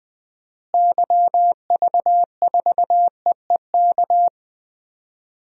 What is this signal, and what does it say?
Morse code, heard as a keyed tone, about 21 wpm, 700 Hz. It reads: YV4EEK
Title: DXCC-100 Call Signs - 6 Characters - Repeated 2x - 20wpm